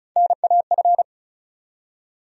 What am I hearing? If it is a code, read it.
Morse code, heard as a keyed tone, about 36 wpm, 700 Hz. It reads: NAF
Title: USA Call Sign Postfixes - 3 Letters 35wpm